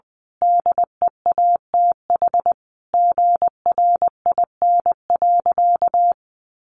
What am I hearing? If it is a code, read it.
Morse code, heard as a keyed tone, about 21 wpm, 700 Hz. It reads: DEATH GRIN.
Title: A Sound Of Thunder 20wpm